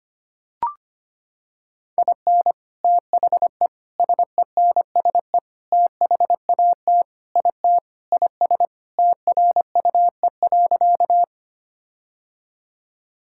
Morse code, heard as a keyed tone, about 25 wpm, 700 Hz. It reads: IN THE SENSE THAT IT IS TRUE.
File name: practice / 25wpm / Sentences with 4-Word Phrases - 25wpm